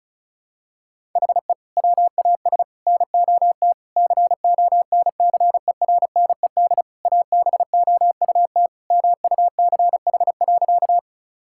Morse code, heard as a keyed tone, about 36 wpm, 700 Hz. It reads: HE WAS NOT CONCERNED ABOUT MUCH.
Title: Sentences from Top 1000 Words - Review Part 6 - 35wpm